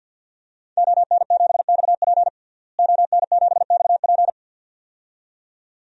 Morse code, heard as a keyed tone, about 48 wpm, 700 Hz. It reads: ON7XP ON7XP
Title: DXCC-100 Call Signs - 5 Characters - Repeated 2x - 50wpm